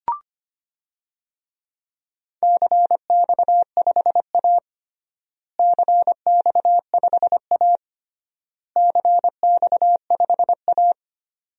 Morse code, heard as a keyed tone, about 25 wpm, 700 Hz. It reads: CX5A CX5A CX5A
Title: DXCC-100 Call Signs - 4 Characters - Repeated 3x - 25wpm